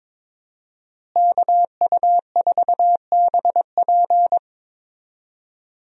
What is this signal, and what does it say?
Morse code, heard as a keyed tone, about 22 wpm, 700 Hz. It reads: KU4BP
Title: USA Call Signs - 5 Characters - Repeated 2x - 22wpm